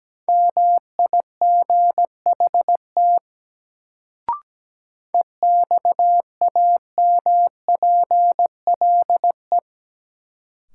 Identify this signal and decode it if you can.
Morse code, heard as a keyed tone, about 17 wpm, 700 Hz. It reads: MIGHT EXAMPLE
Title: Sets of 2 Words - New Word Drill - 17wpm